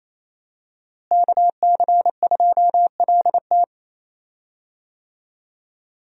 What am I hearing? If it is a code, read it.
Morse code, heard as a keyed tone, about 28 wpm, 700 Hz. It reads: KC2LT
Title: USA Call Signs - 5 Characters - Repeated 2x - 28wpm